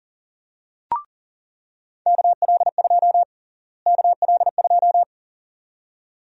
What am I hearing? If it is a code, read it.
Morse code, heard as a keyed tone, about 40 wpm, 700 Hz. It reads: KL2 KL2
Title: USA Call Sign Prefixes - Repeated 2x - 40wpm